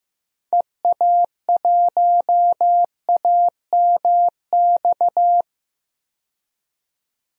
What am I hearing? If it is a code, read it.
Morse code, heard as a keyed tone, about 15 wpm, 700 Hz. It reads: EA1AMX
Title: DXCC-100 Call Signs - 6 Characters - 15wpm